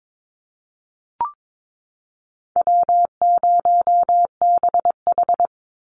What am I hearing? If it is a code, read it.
Morse code, heard as a keyed tone, about 22 wpm, 700 Hz. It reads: W0BH
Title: Easy Call Signs Repeated 2x - 22wpm